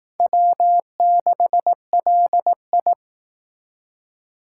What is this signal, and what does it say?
Morse code, heard as a keyed tone, about 18 wpm, 700 Hz. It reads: W6LI
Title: Speed Racing - USA 1x2 Call Signs 15wpm